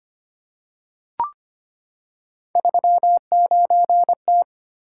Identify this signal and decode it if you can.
Morse code, heard as a keyed tone, about 25 wpm, 700 Hz. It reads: 39T
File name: Three Letter-Number Combinations - 25wpm